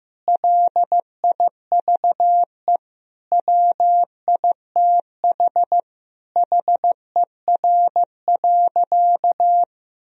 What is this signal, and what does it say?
Morse code, heard as a keyed tone, about 15 wpm, 700 Hz. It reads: LIVE WITH HER.
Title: Sentences from Top 1000 Words - Review Part 3 - 15wpm